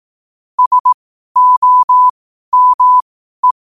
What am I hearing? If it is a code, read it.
Morse code, heard as a keyed tone, about 18 wpm, spaced as for 12 wpm, 1000 Hz. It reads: SOME
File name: qotdfiles / 12wpm